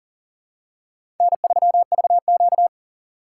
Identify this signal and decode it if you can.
Morse code, heard as a keyed tone, about 39 wpm, 700 Hz. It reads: N3VQ
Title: USA Call Signs - 4 Characters - 6x Character Spacing - 40wpm